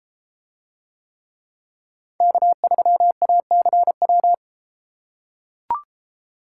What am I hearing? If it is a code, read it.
Morse code, heard as a keyed tone, about 31 wpm, 700 Hz. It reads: K3ACW